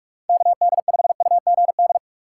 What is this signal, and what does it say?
Morse code, heard as a keyed tone, about 46 wpm, 700 Hz. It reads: KD5UGD
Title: USA Call Signs - 6x Character Spacing - 45wpm